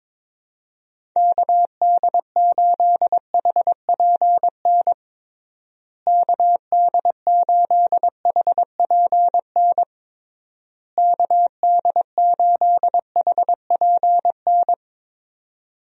Morse code, heard as a keyed tone, about 22 wpm, 700 Hz. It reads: KD8HPN KD8HPN KD8HPN
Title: USA Call Signs - 6 Characters - Repeated 3x - 22wpm